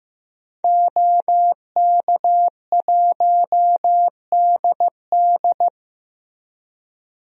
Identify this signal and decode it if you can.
Morse code, heard as a keyed tone, about 15 wpm, 700 Hz. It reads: OK1DD
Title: DXCC-100 Call Signs - 5 Characters - Repeated 2x - 15wpm